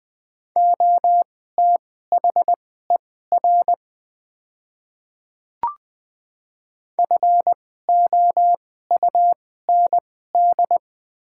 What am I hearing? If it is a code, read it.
Morse code, heard as a keyed tone, about 20 wpm, spaced as for 14 wpm, 700 Hz. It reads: OTHER FOUND
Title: 2x Character Spacing - Top 300 Words - 20wpm